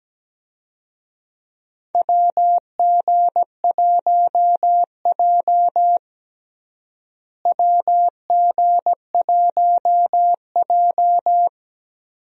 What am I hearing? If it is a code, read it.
Morse code, heard as a keyed tone, about 17 wpm, 700 Hz. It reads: WG1J WG1J
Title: USA Call Signs - 4 Characters - Repeated 3x - 17wpm